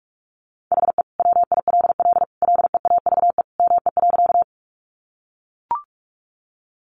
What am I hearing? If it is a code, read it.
Morse code, heard as a keyed tone, about 44 wpm, 700 Hz. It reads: HE WILL LEAVE ME.
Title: Sentences from Top 400 Words 45wpm